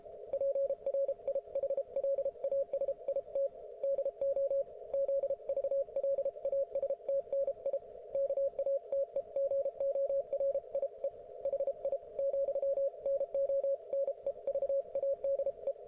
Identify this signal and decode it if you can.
Morse code, heard as a keyed tone, about 33 wpm, 550 Hz. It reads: PRIHLASIT DO ZVLASTNI KATEGORIE HI , NO NEVADE